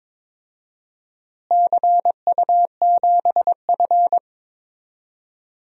Morse code, heard as a keyed tone, about 22 wpm, 700 Hz. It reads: CU7F
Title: DXCC-100 Call Signs - 4 Characters - Repeated 2x - 22wpm